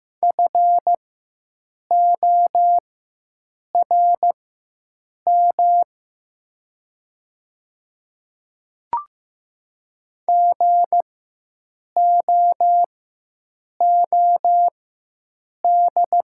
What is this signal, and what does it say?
Morse code, heard as a keyed tone, about 15 wpm, spaced as for 7 wpm, 700 Hz. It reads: FORM GOOD
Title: 4x Character Spacing - Top 200 Words - 15wpm